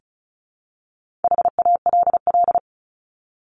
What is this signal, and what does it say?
Morse code, heard as a keyed tone, about 34 wpm, 700 Hz. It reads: HALL
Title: Top 1751-2000 Words 35wpm